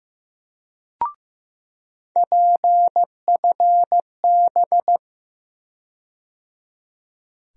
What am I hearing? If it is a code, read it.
Morse code, heard as a keyed tone, about 15 wpm, 700 Hz. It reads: PFB
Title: USA Call Sign Postfixes - 3 Letters 15wpm